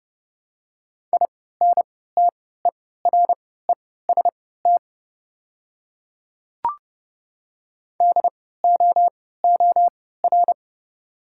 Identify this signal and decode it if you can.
Morse code, heard as a keyed tone, about 30 wpm, spaced as for 17 wpm, 690 Hz. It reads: INTEREST DOOR
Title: 3x Character Spacing - Top 500 Words - 30wpm